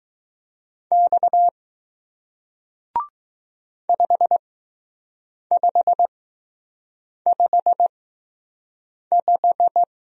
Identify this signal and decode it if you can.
Morse code, heard as a keyed tone, about 20 wpm, spaced as for 6 wpm, 700 Hz. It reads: X 5555